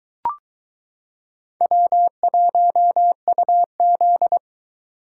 Morse code, heard as a keyed tone, about 22 wpm, 700 Hz. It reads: W1UZ